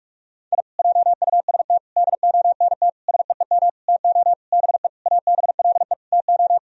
Morse code, heard as a keyed tone, about 44 wpm, 700 Hz. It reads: I JUST DONT SEEM TO BE ABLE TO
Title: Sentences from Top 1000 Words - Review Part 2 - 45wpm